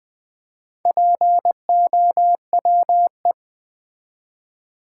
Morse code, heard as a keyed tone, about 20 wpm, 700 Hz. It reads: POWE
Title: Build A Word - 4 Letters At A Time - 8-Letter Words - 20wpm